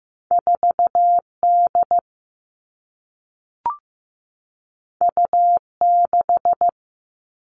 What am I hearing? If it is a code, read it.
Morse code, heard as a keyed tone, about 15 wpm, 700 Hz. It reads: 4 D U 6
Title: Two Challenging Letter-Number Combinations 15wpm